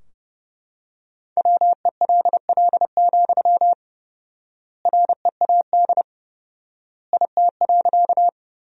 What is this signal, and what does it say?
Morse code, heard as a keyed tone, about 30 wpm, 700 Hz. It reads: WELL, READ IT.
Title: 4x Word-Spacing - Sentences from Top 200 Words - 30wpm